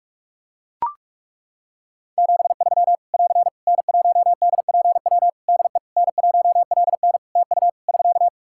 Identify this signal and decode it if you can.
Morse code, heard as a keyed tone, about 41 wpm, 700 Hz. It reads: 73 <AR> N1DPW DE N1LN TU <SK>
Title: Practice QSO - Exchange 4 - Ending - 45wpm